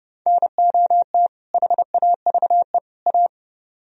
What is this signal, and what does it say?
Morse code, heard as a keyed tone, about 35 wpm, 700 Hz. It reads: NOT HAVE A
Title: Speed-Racing - Sets of 3 Words 30wpm